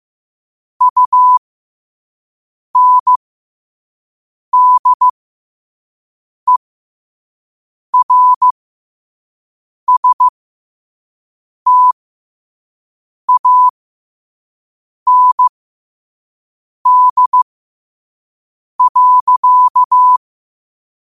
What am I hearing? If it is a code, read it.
Morse code, heard as a keyed tone, about 15 wpm, spaced as for 5 wpm, 1000 Hz. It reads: UNDERSTAND.
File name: qotdfiles / 5wpm